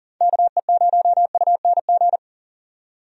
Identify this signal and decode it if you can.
Morse code, heard as a keyed tone, about 40 wpm, 700 Hz. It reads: KE0UNG
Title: USA Call Signs - 6 Characters - 40wpm